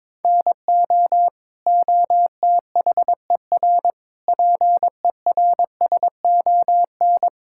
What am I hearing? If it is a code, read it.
Morse code, heard as a keyed tone, about 22 wpm, 700 Hz. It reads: NO OTHER PERSON
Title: Sentences from Top 1000 Words - Review Part 5 - 22wpm